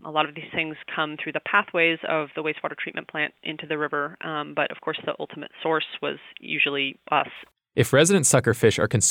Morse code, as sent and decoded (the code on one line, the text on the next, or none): none